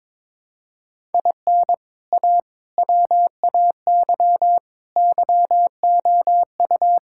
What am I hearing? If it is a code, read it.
Morse code, heard as a keyed tone, about 22 wpm, 700 Hz. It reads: IN A WAY YOU
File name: practice / 22wpm / Sentences with 3-Word Phrases - 22wpm